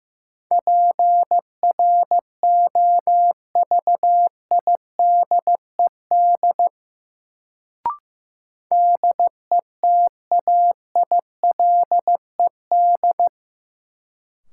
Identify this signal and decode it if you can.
Morse code, heard as a keyed tone, about 15 wpm, 700 Hz. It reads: PROVIDED DETAILED
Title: Sentences from Top 1500 Words - New Word Drill - 15wpm